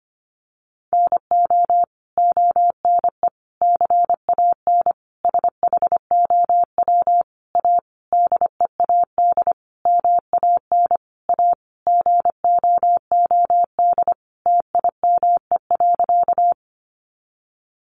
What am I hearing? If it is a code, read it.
Morse code, heard as a keyed tone, about 25 wpm, 700 Hz. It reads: NO ONE CAN SHOW A DEAD MAN A GOOD TIME.
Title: Sentences from Top 1500 Words 25wpm